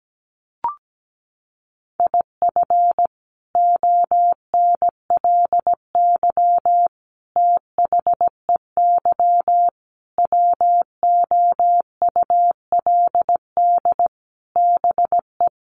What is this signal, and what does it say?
Morse code, heard as a keyed tone, about 17 wpm, 700 Hz. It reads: IF ONLY THEY WOULD BE
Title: Sentences from Top 500 Words 17wpm